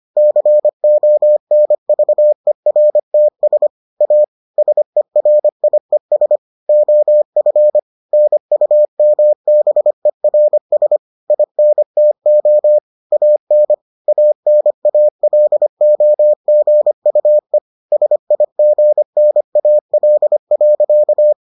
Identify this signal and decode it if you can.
Morse code, heard as a keyed tone, about 25 wpm, 600 Hz. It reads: CONVERTS A SERIES OF NUMBERS INTO AN ANALOGUE SIGNAL.